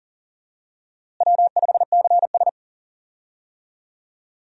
Morse code, heard as a keyed tone, about 40 wpm, 700 Hz. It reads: W5CS
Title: USA Call Signs - 4 Characters - Repeated 2x - 40wpm